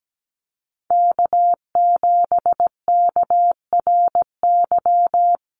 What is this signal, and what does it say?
Morse code, heard as a keyed tone, about 17 wpm, 700 Hz. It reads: K7KRY
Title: USA Call Signs Repeated 2x 17wpm